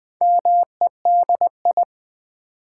Morse code, heard as a keyed tone, about 20 wpm, 700 Hz. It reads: MEDI
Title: Build A Word - 4 Letters At A Time - 10-Letter Words - 20wpm